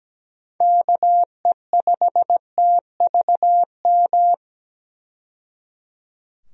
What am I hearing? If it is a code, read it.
Morse code, heard as a keyed tone, about 17 wpm, 700 Hz. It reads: KE5TVM
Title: USA Call Signs - 6 Characters - 17wpm